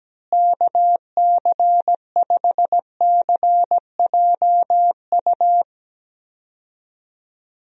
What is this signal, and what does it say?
Morse code, heard as a keyed tone, about 17 wpm, 700 Hz. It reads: KC5CJU
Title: USA Call Signs - 6 Characters - Repeated 3x - 17wpm